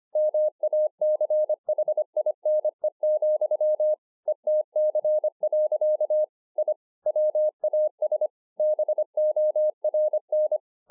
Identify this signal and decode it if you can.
Morse code, heard as a keyed tone, about 25 wpm, 610 Hz. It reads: MACHINE, ETC. I WAS BORN